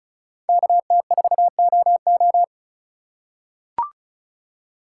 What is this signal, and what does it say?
Morse code, heard as a keyed tone, about 36 wpm, 700 Hz. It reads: KT4OO
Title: USA Call Signs - 4x Character Spacing - 35wpm